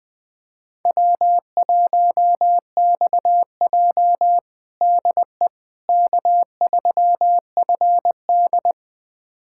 Morse code, heard as a keyed tone, about 20 wpm, 700 Hz. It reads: W1XJ DE K3FD
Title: USA Callsign DE Callsign - 4 Characters - 20wpm